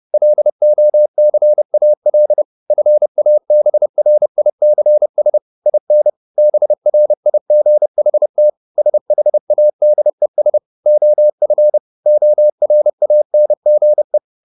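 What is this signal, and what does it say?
Morse code, heard as a keyed tone, about 30 wpm, 600 Hz. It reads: LOCAL FABRICS IN BRIGHT SHADES OF ORANGE
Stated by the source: Morse code at 30 wpm, the novel Never by Ken Follett in English, from CW med Gnister.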